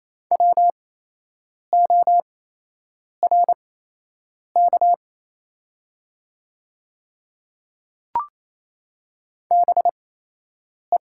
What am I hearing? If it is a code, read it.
Morse code, heard as a keyed tone, about 28 wpm, spaced as for 8 wpm, 700 Hz. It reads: WORK BE
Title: 8x Character Spacing - Top 200 Words - 28wpm